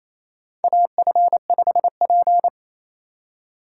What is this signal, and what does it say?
Morse code, heard as a keyed tone, about 28 wpm, 700 Hz. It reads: AF5P
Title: K1USN SST QSO Practice - 28wpm